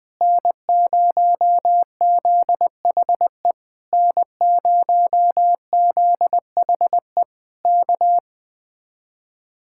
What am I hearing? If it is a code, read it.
Morse code, heard as a keyed tone, about 20 wpm, 700 Hz. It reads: N0ZHE N0ZHE K
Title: Speed-Racing Common QSO Phrases - 17wpm